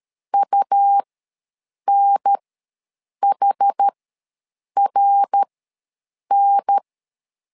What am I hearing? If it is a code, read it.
Morse code, heard as a keyed tone, about 13 wpm, spaced as for 7 wpm, 800 Hz. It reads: UNHRN
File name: LEZIONI / 3 MWRO / 3 MWR ws 12wpm 1809